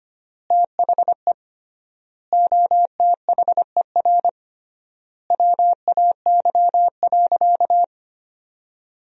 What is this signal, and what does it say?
Morse code, heard as a keyed tone, about 25 wpm, 700 Hz. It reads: THE OTHER WAY.
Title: 3x Word-Spacing - Sentences from Top 100 Words - 25wpm